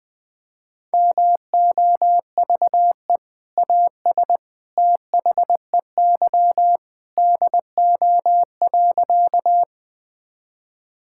Morse code, heard as a keyed tone, about 21 wpm, 700 Hz. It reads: MOVE AS THEY DO.
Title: Sentences with 2-Word Phrases - 20wpm